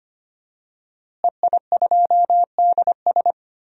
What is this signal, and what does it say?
Morse code, heard as a keyed tone, about 25 wpm, 700 Hz. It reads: EI2DS
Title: DXCC-100 Call Signs - 5 Characters - Repeated 3x - 25wpm